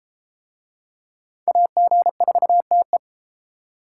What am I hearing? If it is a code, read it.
Morse code, heard as a keyed tone, about 31 wpm, 700 Hz. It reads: AG4TE